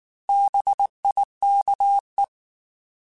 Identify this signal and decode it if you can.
Morse code, heard as a keyed tone, about 19 wpm, 800 Hz. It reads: BIKE
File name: Alphabetical / Alphabetical mp3s / Bs @18